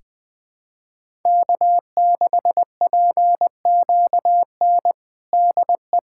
Morse code, heard as a keyed tone, about 20 wpm, 700 Hz. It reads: K6PQN DE
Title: USA Callsign DE Callsign - 5 Characters - 20wpm